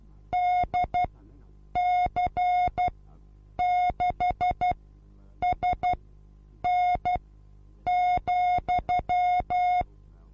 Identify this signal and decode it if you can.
Morse code, heard as a keyed tone, about 12 wpm, spaced as for 8 wpm, 720 Hz. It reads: DC6SN,
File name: Onlinelehrgang / morsen / DJ4UF Morsen23